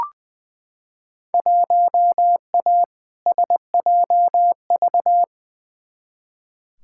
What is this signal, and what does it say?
Morse code, heard as a keyed tone, about 20 wpm, 700 Hz. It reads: 1A SJV
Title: Field Day Exchanges Encore - 20wpm